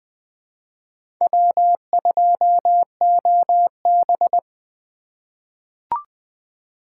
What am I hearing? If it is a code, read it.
Morse code, heard as a keyed tone, about 20 wpm, 700 Hz. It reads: W2OB
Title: USA Call Signs - 4 Characters - 3x Character Spacing - 20wpm